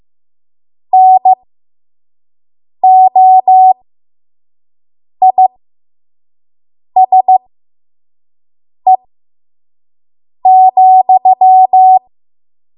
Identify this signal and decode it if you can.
Morse code, heard as a keyed tone, about 15 wpm, spaced as for 5 wpm, 750 Hz. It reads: NOISE,
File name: Morse / 05WPM